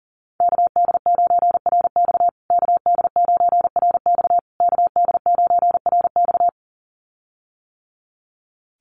Morse code, heard as a keyed tone, about 40 wpm, 700 Hz. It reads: KD9RX KD9RX KD9RX
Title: USA Call Signs Repeated 3x 40wpm